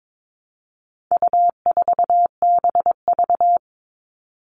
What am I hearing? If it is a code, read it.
Morse code, heard as a keyed tone, about 22 wpm, 700 Hz. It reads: U4BV
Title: All Cut Numbers - 4-Digit 22wpm